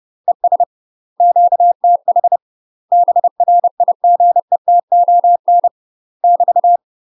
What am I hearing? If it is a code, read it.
Morse code, heard as a keyed tone, about 30 wpm, 700 Hz. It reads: ES QTH BRIGETON <BT>
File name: UK MASTER QSOs TRAINING / 30WPM / UK MASTER QSOs 0044